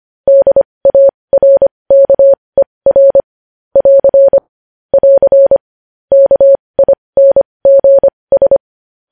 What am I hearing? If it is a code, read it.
Morse code, heard as a keyed tone, about 25 wpm, spaced as for 20 wpm, 550 Hz. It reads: DARKER <AR> <AR> KINGS
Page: News headlines in Morse code at 20 WPM